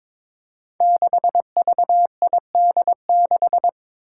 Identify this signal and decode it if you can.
Morse code, heard as a keyed tone, about 22 wpm, 700 Hz. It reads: 6VID6
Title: Five Challenging Letter-Number Combinations 22wpm